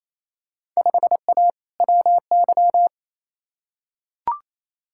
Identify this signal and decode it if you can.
Morse code, heard as a keyed tone, about 28 wpm, 700 Hz. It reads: 5A WY
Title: Field Day Exchanges Encore - 28wpm